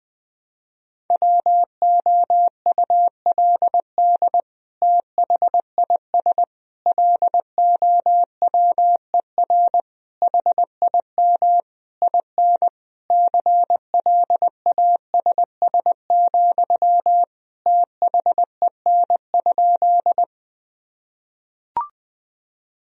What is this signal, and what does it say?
Morse code, heard as a keyed tone, about 20 wpm, 700 Hz. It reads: WOULD THIS LOWER HIM IN CLASS, THEN?
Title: Sentences from Top 1000 Words - Review Part 4 - 20wpm